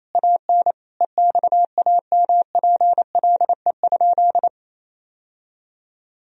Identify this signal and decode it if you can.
Morse code, heard as a keyed tone, about 28 wpm, 700 Hz. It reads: AN EXAMPLE?
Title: Sentences from Top 1000 Words - Review Part 1 - 28wpm